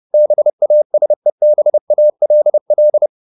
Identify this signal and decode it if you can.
Morse code, heard as a keyed tone, about 30 wpm, 600 Hz. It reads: BASEBALL
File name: the old man and the sea / the old man and the sea 1 30wpm